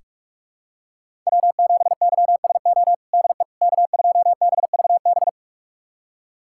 Morse code, heard as a keyed tone, about 47 wpm, 700 Hz. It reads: W7YSO DE K2BVB
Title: USA Callsign DE Callsign - 5 Characters - 45wpm